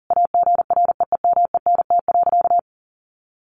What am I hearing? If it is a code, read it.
Morse code, heard as a keyed tone, about 40 wpm, 700 Hz. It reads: AGREEMENT.
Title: Sentences from Top 1500 Words 40wpm